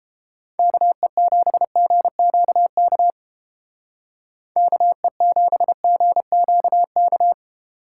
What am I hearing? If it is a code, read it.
Morse code, heard as a keyed tone, about 30 wpm, 700 Hz. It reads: KE7GQK KE7GQK